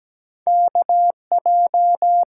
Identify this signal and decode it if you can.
Morse code, heard as a keyed tone, about 17 wpm, 700 Hz. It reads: K J
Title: USA Call Signs - 6 Characters - 8x Character Spacing - 17wpm